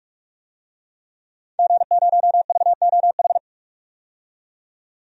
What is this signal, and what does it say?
Morse code, heard as a keyed tone, about 41 wpm, 700 Hz. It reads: G0VOH
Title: DXCC-100 Call Signs - 5 Characters - 45wpm